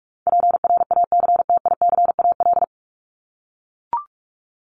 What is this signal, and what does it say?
Morse code, heard as a keyed tone, about 44 wpm, 700 Hz. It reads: PRACTICAL
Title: LY Suffix - Top 100 Words with Base Words 45wpm